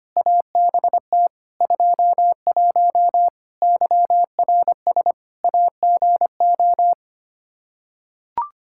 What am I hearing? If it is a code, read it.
Morse code, heard as a keyed tone, about 25 wpm, 700 Hz. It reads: ABT 21 YRS AGO
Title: Practice QSO - Exchange 3 - More Info - 25wpm